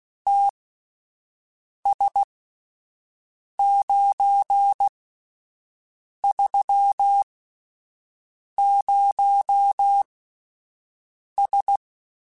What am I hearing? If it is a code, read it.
Morse code, heard as a keyed tone, about 16 wpm, spaced as for 5 wpm, 800 Hz. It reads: TS930S
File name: (spaced) / QSOs Spaced-mp3s / WA7QOX 2 @10 sp